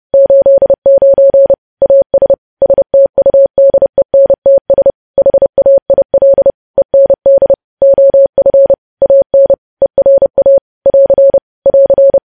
Morse code, heard as a keyed tone, about 30 wpm, 550 Hz. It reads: 89 AS STUDENTS HAIL END OF AN ERA <AR> <AR>
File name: podcast / News headlines-1x-30x30